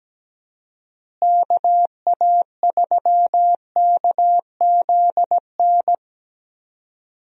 Morse code, heard as a keyed tone, about 17 wpm, 700 Hz. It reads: KA3KZN
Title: USA Call Signs - 6 Characters - Repeated 3x - 17wpm